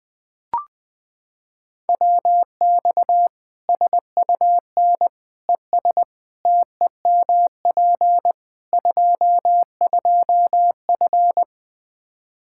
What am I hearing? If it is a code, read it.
Morse code, heard as a keyed tone, about 20 wpm, 700 Hz. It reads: WX SUN ES TEMP 22F
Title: Practice QSO - Exchange 2 - Info - 20wpm